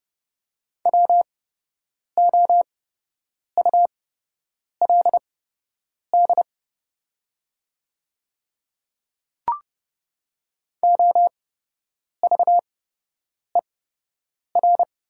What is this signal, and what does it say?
Morse code, heard as a keyed tone, about 30 wpm, spaced as for 8 wpm, 700 Hz. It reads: WOULD OVER
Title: 8x Character Spacing - Top 100 Words - 30wpm